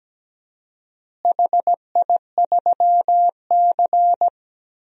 Morse code, heard as a keyed tone, about 17 wpm, 700 Hz. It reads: HI3C
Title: DXCC-100 Call Signs - 4 Characters - Repeated 2x - 17wpm